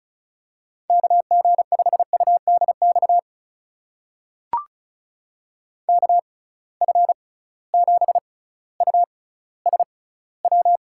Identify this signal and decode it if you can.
Morse code, heard as a keyed tone, about 36 wpm, spaced as for 18 wpm, 690 Hz. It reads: KG5UDX KF7USW
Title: USA Call Signs - 6 Characters - 6x Character Spacing - 35wpm